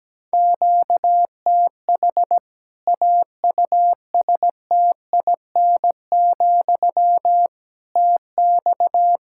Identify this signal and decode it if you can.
Morse code, heard as a keyed tone, about 17 wpm, 700 Hz. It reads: QTH AUSTIN, TX
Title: K1USN SST QSO Practice 17wpm